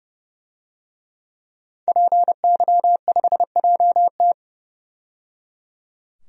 Morse code, heard as a keyed tone, about 30 wpm, 700 Hz. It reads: PY5JT
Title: DXCC-100 Call Signs - 5 Characters - 30wpm